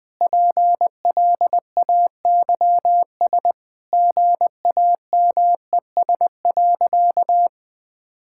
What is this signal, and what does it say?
Morse code, heard as a keyed tone, about 20 wpm, 700 Hz. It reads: PLAYS GAMES.
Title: Easy Sentences with Hints - 20wpm